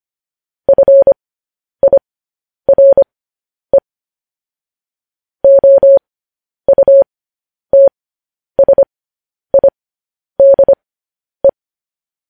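Morse code, heard as a keyed tone, about 25 wpm, spaced as for 10 wpm, 550 Hz. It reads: FIRE OUTSIDE
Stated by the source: Morse code news headlines practice, at 10 WPM.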